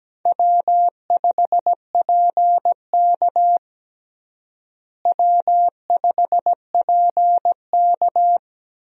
Morse code, heard as a keyed tone, about 17 wpm, 700 Hz. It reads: W5PK W5PK
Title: USA Call Signs - 4 Characters - Repeated 3x - 17wpm